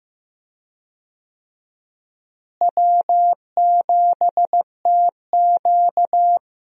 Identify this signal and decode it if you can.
Morse code, heard as a keyed tone, about 15 wpm, 700 Hz. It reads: W7TQ
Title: USA Call Signs - 4 Characters - Repeated 3x - 15wpm